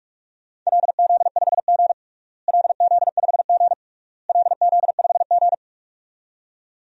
Morse code, heard as a keyed tone, about 45 wpm, 700 Hz. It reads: LZ5G LZ5G LZ5G
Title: DXCC-100 Call Signs - 4 Characters - Repeated 3x - 45wpm